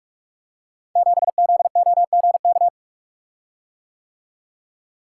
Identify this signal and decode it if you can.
Morse code, heard as a keyed tone, about 44 wpm, 700 Hz. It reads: 7ZOGK
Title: Code Groups - 5 Letter:Numbers - 45wpm